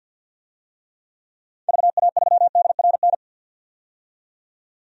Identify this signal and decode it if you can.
Morse code, heard as a keyed tone, about 46 wpm, 700 Hz. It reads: VA3DRN
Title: DXCC-100 Call Signs - 6 Characters - 50wpm